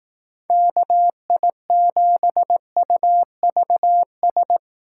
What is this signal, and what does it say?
Morse code, heard as a keyed tone, about 18 wpm, 700 Hz. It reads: KI7UVS
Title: Speed Racing - USA 2x3 Call Signs 15wpm